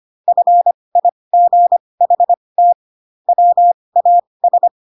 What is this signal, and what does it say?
Morse code, heard as a keyed tone, about 25 wpm, spaced as for 20 wpm, 700 Hz. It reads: FIGHT WAS
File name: Pinocchio / Pinocchio ch2 15 25 ik1hge